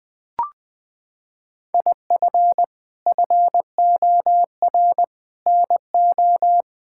Morse code, heard as a keyed tone, about 21 wpm, 700 Hz. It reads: IF FOR NO
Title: Sets of 6 Words - 20wpm